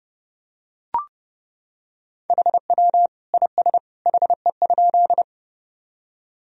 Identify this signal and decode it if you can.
Morse code, heard as a keyed tone, about 30 wpm, 700 Hz. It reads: HW IS HE?
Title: Sentences from Top 100 Words and Abbreviations 30wpm